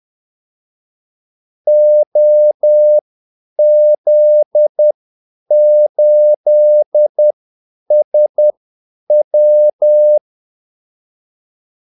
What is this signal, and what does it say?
Morse code, heard as a keyed tone, about 10 wpm, spaced as for 8 wpm, 600 Hz. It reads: OZ8SW
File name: Callsigns-10-8wpm